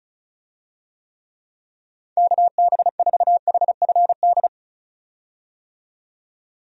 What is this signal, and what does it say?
Morse code, heard as a keyed tone, about 34 wpm, 700 Hz. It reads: KB4HFD
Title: USA Call Signs - 6 Characters - 35wpm